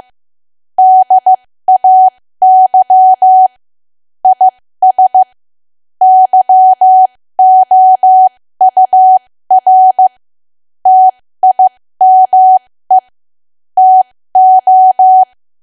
Morse code, heard as a keyed tone, about 15 wpm, spaced as for 13 wpm, 750 Hz. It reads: DAY IS YOUR TIME TO
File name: cwa / W1AW-Level2 / 13WPM